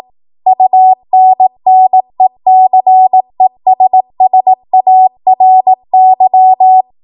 Morse code, heard as a keyed tone, about 18 wpm, 750 Hz. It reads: UNNECESSARY
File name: Morse / 18WPM